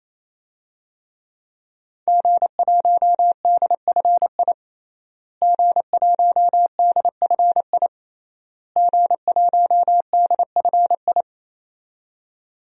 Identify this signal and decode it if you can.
Morse code, heard as a keyed tone, about 28 wpm, 700 Hz. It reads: G1DFI G1DFI G1DFI
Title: DXCC-100 Call Signs - 5 Characters - Repeated 3x - 28wpm